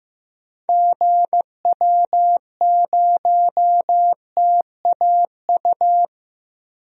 Morse code, heard as a keyed tone, about 16 wpm, 700 Hz. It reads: GW0TAU
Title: DXCC-100 Call Signs - 6 Characters - Repeated 3x - 15wpm